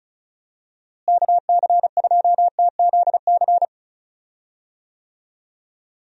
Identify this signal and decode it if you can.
Morse code, heard as a keyed tone, about 36 wpm, 700 Hz. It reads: KC2TZC
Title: USA Call Signs - 6 Characters - Repeated 2x - 35wpm